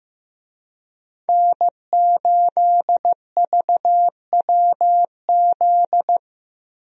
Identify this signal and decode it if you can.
Morse code, heard as a keyed tone, about 16 wpm, 700 Hz. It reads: N8VWZ
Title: USA Call Signs - 5 Characters - Repeated 3x - 15wpm